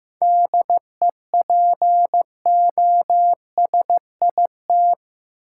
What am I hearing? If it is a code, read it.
Morse code, heard as a keyed tone, about 15 wpm, 700 Hz. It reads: DEPOSIT
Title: Sentences from Top 1300 Words - New Word Drill Alt - 15wpm